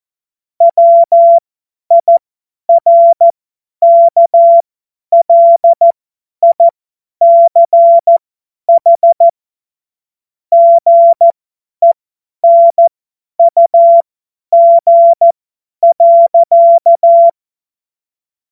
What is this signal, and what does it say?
Morse code, heard as a keyed tone, about 14 wpm, spaced as for 10 wpm, 680 Hz. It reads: WIRKLICH GENUG.
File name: Morsekurs / Witze / Witz 25. Januar 2022, 50 F70